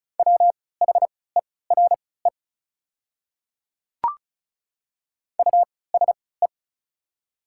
Morse code, heard as a keyed tone, about 34 wpm, spaced as for 20 wpm, 700 Hz. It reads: WHERE USE
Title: 3x Character Spacing - Top 300 Words - 35wpm